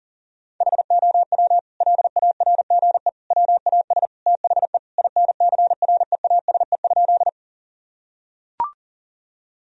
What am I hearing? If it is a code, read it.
Morse code, heard as a keyed tone, about 37 wpm, 700 Hz. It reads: HOW LARGE WAS THE INCREASE?
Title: Sentences from Top 1000 Words - Review Part 4 - 40wpm